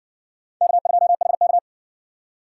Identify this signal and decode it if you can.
Morse code, heard as a keyed tone, about 53 wpm, 700 Hz. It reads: K2HK